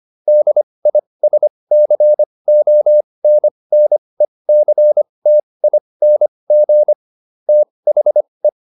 Morse code, heard as a keyed tone, about 25 wpm, spaced as for 20 wpm, 600 Hz. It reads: DISCONNECTING THE